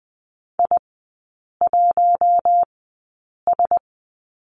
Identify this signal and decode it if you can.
Morse code, heard as a keyed tone, about 20 wpm, spaced as for 8 wpm, 700 Hz. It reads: I1S
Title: 087 Lesson 15e - Number 5 - Three Characters 20wpm